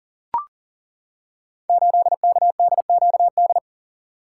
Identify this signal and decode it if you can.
Morse code, heard as a keyed tone, about 42 wpm, 700 Hz. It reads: 8KDQD
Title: Code Groups - 5 Letter:Numbers - 40wpm